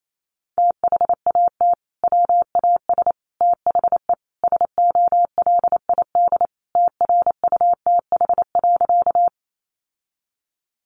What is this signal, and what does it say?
Morse code, heard as a keyed tone, about 28 wpm, 700 Hz. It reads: THAT WAS THE SOLID TRUTH.
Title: Sentences from Top 1500 Words 28wpm